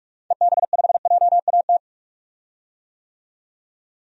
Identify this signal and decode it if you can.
Morse code, heard as a keyed tone, about 46 wpm, 700 Hz. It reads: EB5JAT
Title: DXCC-100 Call Signs - 6 Characters - 45wpm